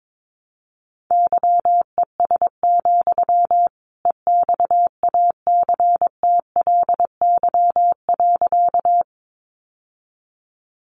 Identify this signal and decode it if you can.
Morse code, heard as a keyed tone, about 22 wpm, 700 Hz. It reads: YES, EXACTLY.
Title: Sentences from Top 900 Words 22wpm